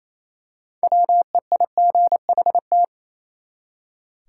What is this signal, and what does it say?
Morse code, heard as a keyed tone, about 28 wpm, 700 Hz. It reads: WEIGHT
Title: Six Letter Words - 28wpm